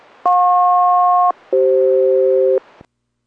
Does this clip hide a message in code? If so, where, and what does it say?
Yes: aircraft SELCAL from 0.3 to 2.6 s, heard as two pairs of tones. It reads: JP-CF